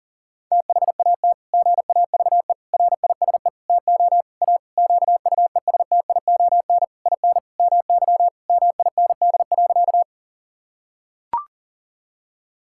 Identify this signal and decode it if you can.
Morse code, heard as a keyed tone, about 40 wpm, 700 Hz. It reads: THAT GAVE RISE TO A QUESTION IN MY MIND.
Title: Sentences from Top 1000 Words - Review Part 6 - 40wpm